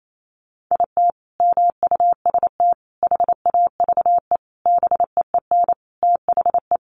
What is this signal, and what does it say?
Morse code, heard as a keyed tone, about 28 wpm, 700 Hz. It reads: IT MUST HAVE BEEN THE
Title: Sentences from Top 800 Words 28wpm